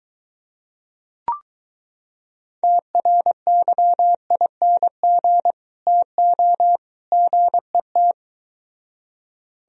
Morse code, heard as a keyed tone, about 24 wpm, 700 Hz. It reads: TRYING TO GET